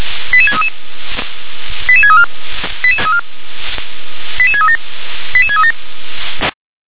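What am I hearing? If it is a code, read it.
Five-tone selective call (ZVEI), heard as quick runs of five tones, tones of about 70 ms. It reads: 88633 88634 88637 88637
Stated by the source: ZVEI five-tone selective call 88631 received on SDR.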